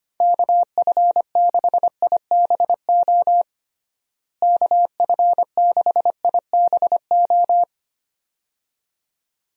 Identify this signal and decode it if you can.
Morse code, heard as a keyed tone, about 25 wpm, 700 Hz. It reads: KF6IBO KF6IBO
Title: USA Call Signs - 6 Characters - Repeated 2x - 25wpm